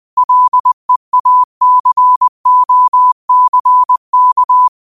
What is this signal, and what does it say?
Morse code, heard as a keyed tone, about 20 wpm, 1000 Hz. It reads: LEACOCK
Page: Daily quotes in Morse Code at 20 Words per Minute. Use this to improve your code copying skill.